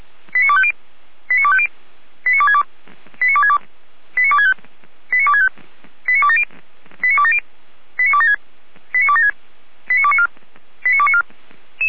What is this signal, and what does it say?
Five-tone selective call (ZVEI), heard as quick runs of five tones, tones of about 70 ms. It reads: 89250 89250 89262 89262 89265 89265 89270 89270 89276 89276 89294 89294